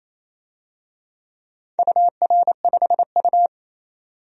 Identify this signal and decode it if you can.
Morse code, heard as a keyed tone, about 28 wpm, 700 Hz. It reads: UR5U
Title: DXCC-100 Call Signs - 4 Characters - 28wpm